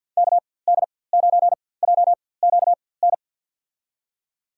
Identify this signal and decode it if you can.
Morse code, heard as a keyed tone, about 50 wpm, spaced as for 23 wpm, 700 Hz. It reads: KD9JQN
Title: USA Call Signs - 6 Characters - 4x Character Spacing - 50wpm